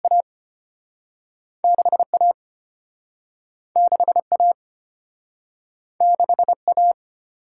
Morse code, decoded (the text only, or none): A6 A6 A6 A